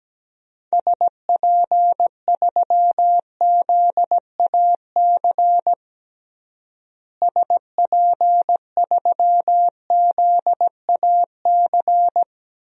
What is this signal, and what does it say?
Morse code, heard as a keyed tone, about 17 wpm, 700 Hz. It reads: SP3ZAC SP3ZAC
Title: DXCC-100 Call Signs - 6 Characters - Repeated 3x - 17wpm